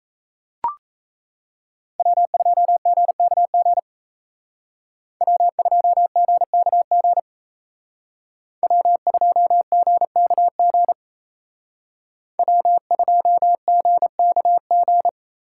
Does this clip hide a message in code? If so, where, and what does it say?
Yes: Morse code from 2.0 to 15.1 s, heard as a keyed tone, about 36 wpm, 700 Hz. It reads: W2GKG W2GKG W2GKG W2GKG